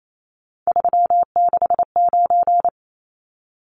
Morse code, heard as a keyed tone, about 28 wpm, 700 Hz. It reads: 369
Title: Numbers - 3-Digit 28wpm